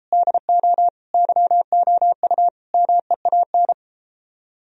Morse code, heard as a keyed tone, about 33 wpm, 700 Hz. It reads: DO YOU MEAN